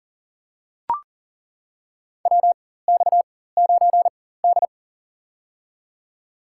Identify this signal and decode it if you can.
Morse code, heard as a keyed tone, about 40 wpm, spaced as for 19 wpm, 700 Hz. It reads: WX9D
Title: USA Call Signs - 4x Character Spacing - 40wpm